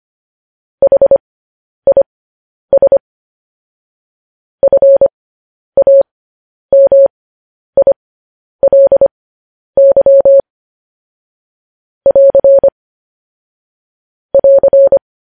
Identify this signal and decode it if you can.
Morse code, heard as a keyed tone, about 25 wpm, spaced as for 10 wpm, 550 Hz. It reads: HIS FAMILY <AR> <AR>
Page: news-headlines-in-morse-code-at-10-wpm